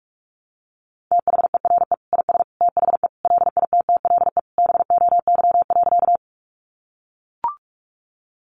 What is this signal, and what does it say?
Morse code, heard as a keyed tone, about 44 wpm, 700 Hz. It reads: THERE IS THE LITTLE BOY.
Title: Sentences from Top 200 Words 45wpm